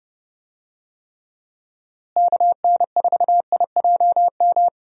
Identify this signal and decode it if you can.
Morse code, heard as a keyed tone, about 31 wpm, 700 Hz. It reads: KN4IJM
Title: USA Call Signs - 6 Characters - Repeated 2x - 30wpm